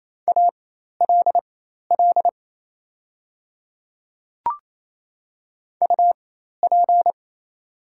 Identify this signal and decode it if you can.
Morse code, heard as a keyed tone, about 27 wpm, spaced as for 13 wpm, 700 Hz. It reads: ALL UP
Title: 4x Character Spacing - Top 100 Words - 28wpm